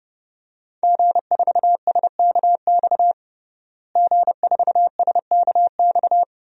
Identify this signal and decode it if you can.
Morse code, heard as a keyed tone, about 32 wpm, 700 Hz. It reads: G4SKX G4SKX
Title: DXCC-100 Call Signs - 5 Characters - Repeated 2x - 30wpm